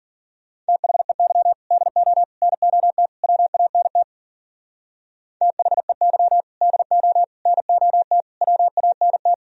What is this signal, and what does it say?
Morse code, heard as a keyed tone, about 50 wpm, 700 Hz. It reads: THEY DO NOT WANT THEY DO NOT WANT
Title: Speed-Racing - Sets of 4 Words 40wpm